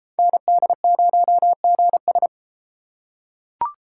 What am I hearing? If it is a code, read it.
Morse code, heard as a keyed tone, about 33 wpm, 700 Hz. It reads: ND0GS